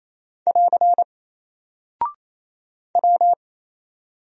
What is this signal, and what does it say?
Morse code, heard as a keyed tone, about 28 wpm, 700 Hz. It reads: <AR> W
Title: Single Character (Letter:Number:Punctuation:Prosign) Alt 28wpm